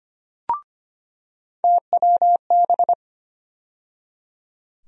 Morse code, heard as a keyed tone, about 25 wpm, 700 Hz. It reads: TWB
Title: USA Call Sign Postfixes - 3 Letters 25wpm